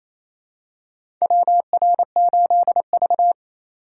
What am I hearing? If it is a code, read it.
Morse code, heard as a keyed tone, about 28 wpm, 700 Hz. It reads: WR8V
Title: USA Call Signs - 4 Characters - 2x Character Spacing - 28wpm